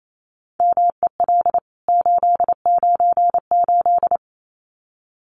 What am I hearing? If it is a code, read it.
Morse code, heard as a keyed tone, about 28 wpm, 700 Hz. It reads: MEL 898
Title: CWT Exchange 28wpm